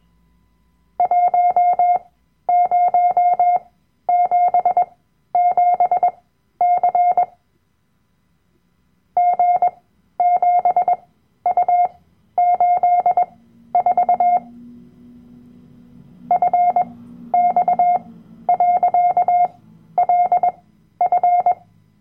Morse code, heard as a keyed tone, about 21 wpm, spaced as for 12 wpm, 710 Hz. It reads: 1077C G7U84 FX.LF
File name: Workshop Dateien / CW-Online-Kurs / Lektion 010